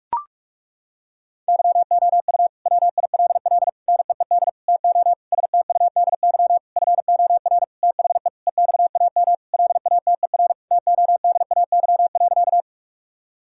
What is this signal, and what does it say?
Morse code, heard as a keyed tone, about 42 wpm, 700 Hz. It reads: YOU WILL NEED TO STUDY FOR THE EXAM LATER TODAY.
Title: Sentences from Top 1000 Words - Review Part 5 - 45wpm